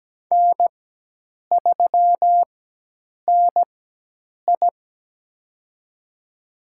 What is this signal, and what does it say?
Morse code, heard as a keyed tone, about 17 wpm, spaced as for 8 wpm, 700 Hz. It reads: N3NI
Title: USA Call Signs - 4x Character Spacing - 17wpm